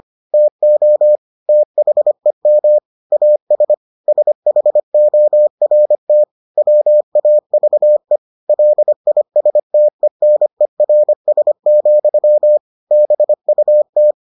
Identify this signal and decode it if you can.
Morse code, heard as a keyed tone, about 25 wpm, 600 Hz. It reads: TO THEM AS SHORT WAVE LISTENERS, BUT